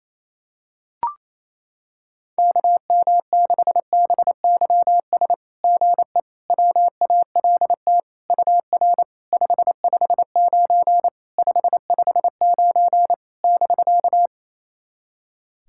Morse code, heard as a keyed tone, about 27 wpm, 690 Hz. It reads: KM6BYS GE WALT UR 559 559 <BK>
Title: SOTA QSO Practice 28wpm